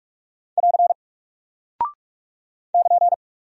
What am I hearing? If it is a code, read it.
Morse code, heard as a keyed tone, about 44 wpm, 700 Hz. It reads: <AR> <KN>
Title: Single Character (Letter:Number:Punctuation:Prosign) Alt 45wpm